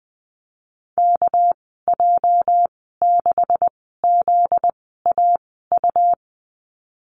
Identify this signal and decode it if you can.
Morse code, heard as a keyed tone, about 20 wpm, spaced as for 14 wpm, 700 Hz. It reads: KJ6ZAU
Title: 245 Lesson 39g - Letter z - Call Signs 20wpm